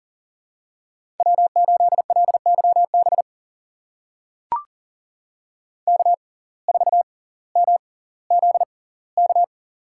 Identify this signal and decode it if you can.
Morse code, heard as a keyed tone, about 38 wpm, spaced as for 20 wpm, 700 Hz. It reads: W8LYB K4MZK
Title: USA Call Signs - 5 Characters - 6x Character Spacing - 40wpm